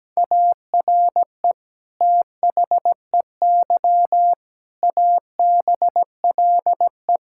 Morse code, heard as a keyed tone, about 17 wpm, 700 Hz. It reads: ARE THEY ABLE
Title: Sentences from Top 1000 Words - Review Part 5 - 17wpm